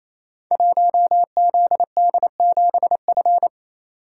Morse code, heard as a keyed tone, about 27 wpm, 700 Hz. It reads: 1ZD7F
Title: Code Groups - 5 Letter:Numbers - 28wpm